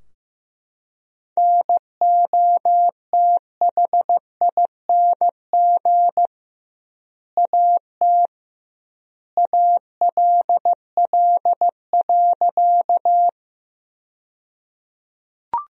Morse code, heard as a keyed tone, about 15 wpm, 700 Hz. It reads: NOTHING AT ALL.
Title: 2x Word-Spacing - Sentences from Top 500 Words - 15wpm